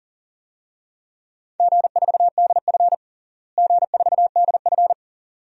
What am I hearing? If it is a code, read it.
Morse code, heard as a keyed tone, about 40 wpm, 700 Hz. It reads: G4DF G4DF
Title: DXCC-100 Call Signs - 4 Characters - Repeated 2x - 40wpm